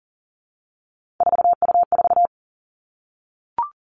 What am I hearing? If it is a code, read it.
Morse code, heard as a keyed tone, about 40 wpm, 700 Hz. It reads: 4U4
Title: Three Challenging Letter-Number Combinations 40wpm